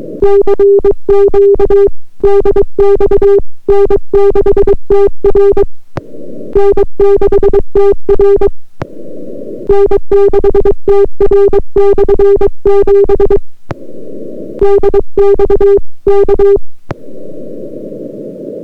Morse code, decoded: CQ DX N6TR N6TR N6TR/7 DX K